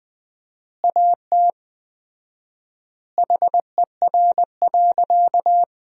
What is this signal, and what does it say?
Morse code, heard as a keyed tone, about 21 wpm, 700 Hz. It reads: AT HER.
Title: 4x Word-Spacing - Sentences from Top 200 Words - 20wpm